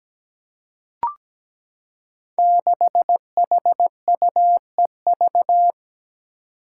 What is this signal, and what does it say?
Morse code, heard as a keyed tone, about 17 wpm, 700 Hz. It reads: 6HUEV
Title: Five Challenging Letter-Number Combinations 17wpm